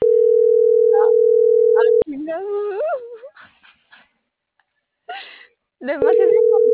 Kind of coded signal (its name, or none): telephone call progress tone